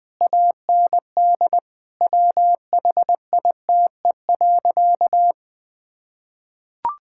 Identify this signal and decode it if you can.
Morse code, heard as a keyed tone, about 20 wpm, 700 Hz. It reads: AND WHITE.
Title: Sentences with 5-Word Phrases - Spoken Beginning - 20wpm